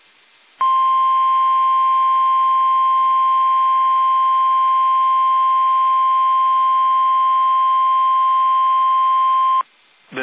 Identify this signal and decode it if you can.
Emergency Alert System (SAME) tones (the 1050 Hz attention tone). It is heard from 0.5 to 9.7 s.
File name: eas / KUSO / KUSO